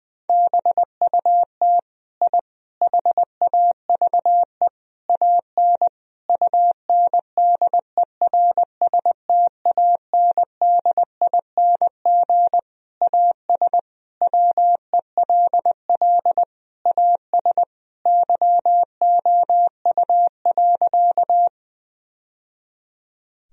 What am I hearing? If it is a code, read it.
Morse code, heard as a keyed tone, about 21 wpm, 700 Hz. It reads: BUT I HAVE AN UNDERSTANDING AS WELL AS YOU.
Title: Sentences from Top 1000 Words - Review Part 3 - 20wpm